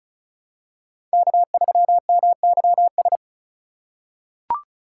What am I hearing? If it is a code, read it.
Morse code, heard as a keyed tone, about 34 wpm, 700 Hz. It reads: K3MYS
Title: USA Call Signs - 5 Characters - 2x Character Spacing - 35wpm